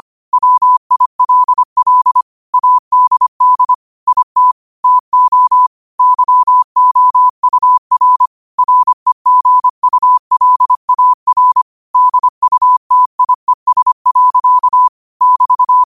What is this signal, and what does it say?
Morse code, heard as a keyed tone, about 25 wpm, 1000 Hz. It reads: WILL ADD IT TO YOUR REGULAR DUTIES. <BT>